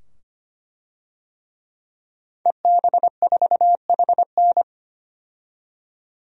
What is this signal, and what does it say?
Morse code, heard as a keyed tone, about 25 wpm, 700 Hz. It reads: EB4HN
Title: DXCC-100 Call Signs - 5 Characters - 25wpm